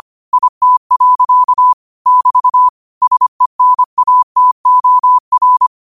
Morse code, heard as a keyed tone, about 25 wpm, 1000 Hz. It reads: IT. <BT> SENATOR